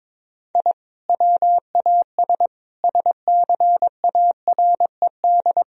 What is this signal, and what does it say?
Morse code, heard as a keyed tone, about 22 wpm, 700 Hz. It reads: I WAS SCARED
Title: Sentences from Top 1000 Words - Review Part 4 - 22wpm